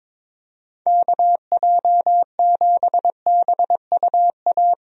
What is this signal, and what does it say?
Morse code, heard as a keyed tone, about 22 wpm, 700 Hz. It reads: KJ7BUA
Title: USA Call Signs - 6 Characters - Repeated 3x - 22wpm